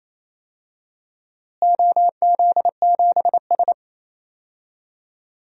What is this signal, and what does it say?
Morse code, heard as a keyed tone, about 28 wpm, 700 Hz. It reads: OZ7S
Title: DXCC-100 Call Signs - 4 Characters - Repeated 2x - 28wpm